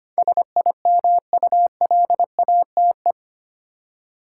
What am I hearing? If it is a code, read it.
Morse code, heard as a keyed tone, about 25 wpm, 700 Hz. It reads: SIMULATE
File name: Eight Letter Words - 25wpm